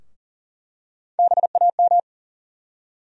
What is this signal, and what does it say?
Morse code, heard as a keyed tone, about 40 wpm, 700 Hz. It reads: BAM
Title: USA Call Sign Postfixes - 3 Letters 40wpm